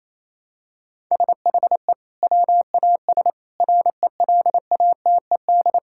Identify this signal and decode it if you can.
Morse code, heard as a keyed tone, about 28 wpm, 700 Hz. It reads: SHE WAS RELATED
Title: Sentences from Top 1000 Words - Review Part 5 - 28wpm